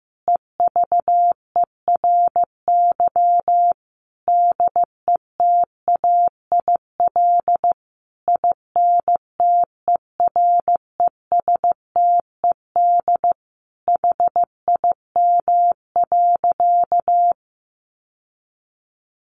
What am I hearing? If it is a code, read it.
Morse code, heard as a keyed tone, about 14 wpm, 700 Hz. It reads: EVERY DETAIL INTERESTED HIM.
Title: Sentences from Top 1500 Words 15wpm